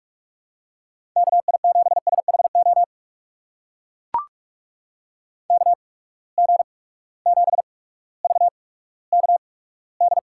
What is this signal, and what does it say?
Morse code, heard as a keyed tone, about 44 wpm, spaced as for 19 wpm, 700 Hz. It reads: KI7SHO KG7VKD
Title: USA Call Signs - 6 Characters - 8x Character Spacing - 45wpm